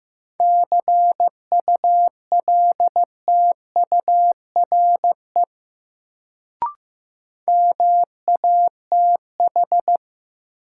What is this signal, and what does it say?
Morse code, heard as a keyed tone, about 15 wpm, 700 Hz. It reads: CULTURE MATH
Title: Sentences from Top 1000 Words - Review Part 6 - New Word Drill - 15wpm